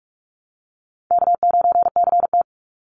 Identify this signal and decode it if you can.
Morse code, heard as a keyed tone, about 44 wpm, 700 Hz. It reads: K9CT
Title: Call Signs - 45wpm